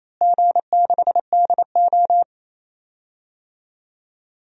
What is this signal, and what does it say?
Morse code, heard as a keyed tone, about 28 wpm, 700 Hz. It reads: G6DO
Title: DXCC-100 Call Signs - 4 Characters - Repeated 2x - 28wpm